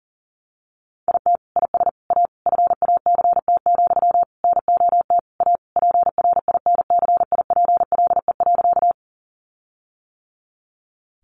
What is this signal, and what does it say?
Morse code, heard as a keyed tone, about 40 wpm, 700 Hz. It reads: IT IS A FACT, NOT A PRINCIPLE.
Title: Sentences from Top 1400 Words 40wpm